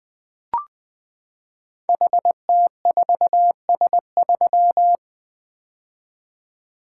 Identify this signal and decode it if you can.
Morse code, heard as a keyed tone, about 20 wpm, 700 Hz. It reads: HT4S3
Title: Code Groups - 5 Letter:Numbers - 20wpm